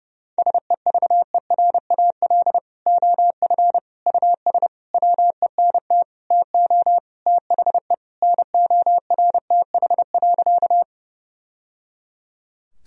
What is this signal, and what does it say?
Morse code, heard as a keyed tone, about 30 wpm, 700 Hz. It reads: SEVERAL OF US WENT TO THE NORTH.
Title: Sentences from Top 1000 Words - Review Part 2 - 30wpm